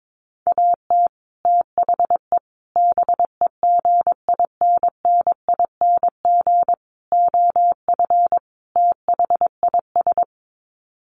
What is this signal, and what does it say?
Morse code, heard as a keyed tone, about 22 wpm, 700 Hz. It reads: AT THE BEGINNING OF THIS
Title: Sets of 5 Words Encore 22wpm